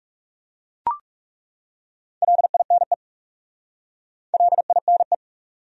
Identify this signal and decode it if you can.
Morse code, heard as a keyed tone, about 37 wpm, 700 Hz. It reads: LINE LINE
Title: Speed-Racing Top 300 Words 30wpm